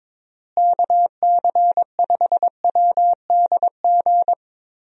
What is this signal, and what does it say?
Morse code, heard as a keyed tone, about 22 wpm, 700 Hz. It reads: KC5WDG
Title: USA Call Signs - 6 Characters - Repeated 3x - 22wpm